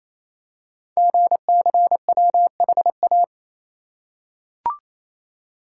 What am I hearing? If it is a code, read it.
Morse code, heard as a keyed tone, about 27 wpm, 700 Hz. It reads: GCWHA
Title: Code Groups - 5 Letter:Numbers - 28wpm